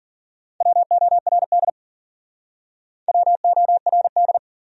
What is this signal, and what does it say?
Morse code, heard as a keyed tone, about 53 wpm, 700 Hz. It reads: WORD WORD